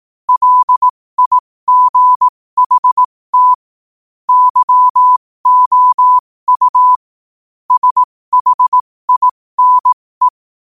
Morse code, heard as a keyed tone, about 18 wpm, spaced as for 15 wpm, 1000 Hz. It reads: LIGHT YOU SHINE
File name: qotdfiles / 15wpm